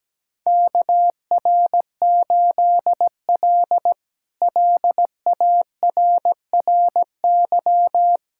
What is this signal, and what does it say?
Morse code, heard as a keyed tone, about 17 wpm, 700 Hz. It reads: KR8L LARRY
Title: K1USN SST QSO Practice - 17wpm